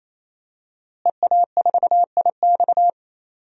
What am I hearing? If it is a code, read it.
Morse code, heard as a keyed tone, about 28 wpm, 700 Hz. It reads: EA4IX
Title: DXCC-100 Call Signs - 5 Characters - Repeated 3x - 28wpm